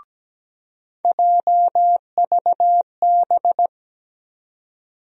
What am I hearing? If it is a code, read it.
Morse code, heard as a keyed tone, about 17 wpm, 700 Hz. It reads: JVB
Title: USA Call Sign Postfixes - 3 Letters 17wpm